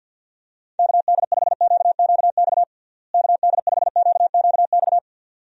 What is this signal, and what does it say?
Morse code, heard as a keyed tone, about 47 wpm, 700 Hz. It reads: KD5QQX KD5QQX
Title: USA Call Signs - 6 Characters - Repeated 2x - 50wpm